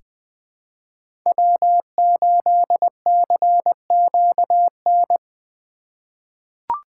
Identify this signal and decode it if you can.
Morse code, heard as a keyed tone, about 20 wpm, 700 Hz. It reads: W8CQN
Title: USA Call Signs - 2x Character Spacing - 20wpm